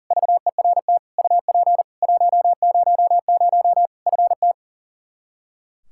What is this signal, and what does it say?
Morse code, heard as a keyed tone, about 40 wpm, 700 Hz. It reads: VERT UP 100 FT
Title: Practice QSO - Exchange 2 - Info - 40wpm